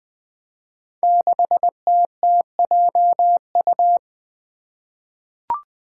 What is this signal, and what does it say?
Morse code, heard as a keyed tone, about 20 wpm, 700 Hz. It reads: 6TTJU
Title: Five Challenging Letter-Number Combinations 20wpm